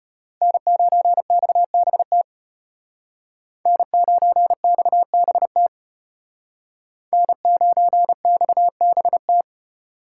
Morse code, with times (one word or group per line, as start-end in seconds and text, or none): 0.4-2.2 N9XBT
3.7-5.7 N9XBT
7.1-9.4 N9XBT